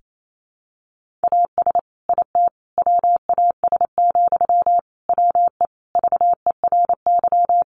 Morse code, heard as a keyed tone, about 28 wpm, 700 Hz. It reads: AS IT WAS, WE VERY
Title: Sentences from Top 1400 Words 28wpm